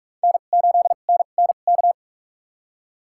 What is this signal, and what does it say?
Morse code, heard as a keyed tone, about 43 wpm, spaced as for 32 wpm, 700 Hz. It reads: N8NNK
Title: USA Call Signs - 5 Characters - 2x Character Spacing - 45wpm